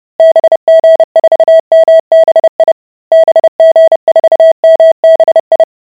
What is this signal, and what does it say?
Morse code, heard as a keyed tone, about 30 wpm, 650 Hz. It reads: BG4MBI BG4MBI